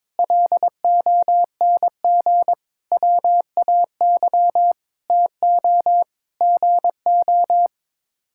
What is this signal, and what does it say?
Morse code, heard as a keyed tone, about 22 wpm, 700 Hz. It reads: LONG WAY TO GO
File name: practice / 22wpm / Sets of 7 Words - 22wpm